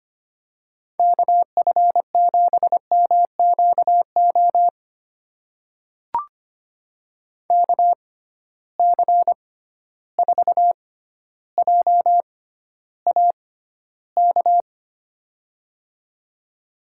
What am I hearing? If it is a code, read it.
Morse code, heard as a keyed tone, about 25 wpm, spaced as for 13 wpm, 700 Hz. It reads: KF7MQO KC4JAK
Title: USA Call Signs - 6 Characters - 6x Character Spacing - 25wpm